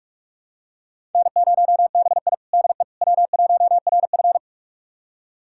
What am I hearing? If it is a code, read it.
Morse code, heard as a keyed tone, about 46 wpm, 700 Hz. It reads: N0BI DE W1RF
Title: USA Callsign DE Callsign - 4 Characters - 45wpm